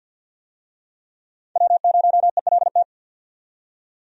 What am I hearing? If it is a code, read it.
Morse code, heard as a keyed tone, about 52 wpm, 700 Hz. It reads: W0ELT
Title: USA Call Signs - 5 Characters - 50wpm